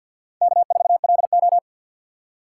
Morse code, heard as a keyed tone, about 49 wpm, 700 Hz. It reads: K4LO
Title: USA Call Signs - 4 Characters - 2x Character Spacing - 50wpm